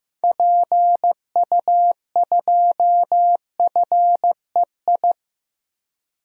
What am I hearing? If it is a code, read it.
Morse code, heard as a keyed tone, about 15 wpm, 700 Hz. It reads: PU2FEI
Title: DXCC-100 Call Signs - 6 Characters - Repeated 3x - 15wpm